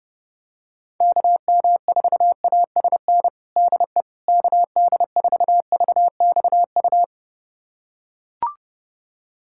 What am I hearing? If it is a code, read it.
Morse code, heard as a keyed tone, about 30 wpm, 700 Hz. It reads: KM4ASN DE KD4VXU
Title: USA Callsign DE Callsign - 6 Characters - 30wpm